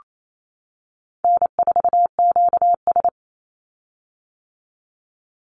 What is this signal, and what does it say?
Morse code, heard as a keyed tone, about 28 wpm, 700 Hz. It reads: N4QS
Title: Easy Call Signs - 28wpm